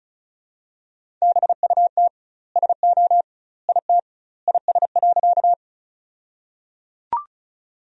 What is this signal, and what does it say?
Morse code, heard as a keyed tone, about 36 wpm, 700 Hz. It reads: BUT SO IT IS.
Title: 2x Word-Spacing - Sentences from Top 100 Words - 35wpm